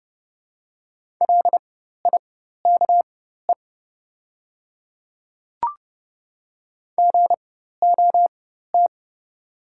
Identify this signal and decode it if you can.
Morse code, heard as a keyed tone, about 30 wpm, spaced as for 14 wpm, 700 Hz. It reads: LIKE GOT
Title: 4x Character Spacing - Top 300 Words - 30wpm